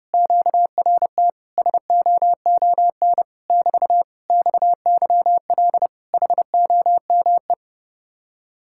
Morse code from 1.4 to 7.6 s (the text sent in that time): SOON <BT> XYL HOME